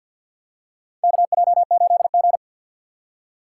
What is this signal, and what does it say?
Morse code, heard as a keyed tone, about 47 wpm, 690 Hz. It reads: KJ8G
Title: USA Call Signs - 2x Character Spacing - 50wpm